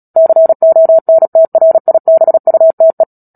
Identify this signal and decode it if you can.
Morse code, heard as a keyed tone, about 37 wpm, 650 Hz. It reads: CONTRIBUTE
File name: Book Two / 10-Chapter 10 - Epilogue - 36 WPM